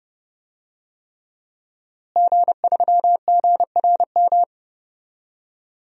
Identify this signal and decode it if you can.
Morse code, heard as a keyed tone, about 30 wpm, 700 Hz. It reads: G3GRM
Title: DXCC-100 Call Signs - 5 Characters - Repeated 3x - 30wpm